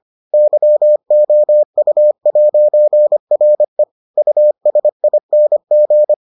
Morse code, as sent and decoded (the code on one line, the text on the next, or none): -.-- --- ..- .----. .-. / ..- ... .. -. --.
YOU'RE USING